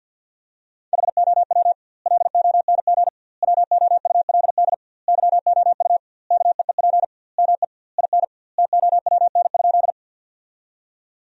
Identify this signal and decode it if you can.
Morse code, heard as a keyed tone, about 51 wpm, 700 Hz. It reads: HOW LONG WOULD YOU KEEP ME IN TOWN?
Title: 2x Word-Spacing - Sentences from Top 500 Words - 50wpm